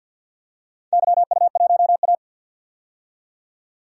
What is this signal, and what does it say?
Morse code, heard as a keyed tone, about 50 wpm, 700 Hz. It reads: YU1A
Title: DXCC-100 Call Signs - 4 Characters - 50wpm